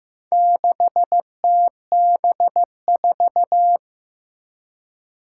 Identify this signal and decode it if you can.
Morse code, heard as a keyed tone, about 15 wpm, 700 Hz. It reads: 6TB4
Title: Four Challenging Letter-Number Combinations 15wpm